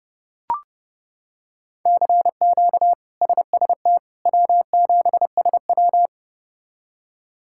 Morse code, heard as a keyed tone, about 32 wpm, 700 Hz. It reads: CQ SST W7SW
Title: K1USN SST QSO Practice - 30wpm